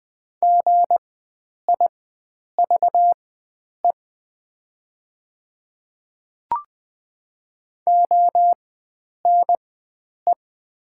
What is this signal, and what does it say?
Morse code, heard as a keyed tone, about 20 wpm, spaced as for 9 wpm, 700 Hz. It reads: GIVE ONE
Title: 4x Character Spacing - Top 500 Words - 20wpm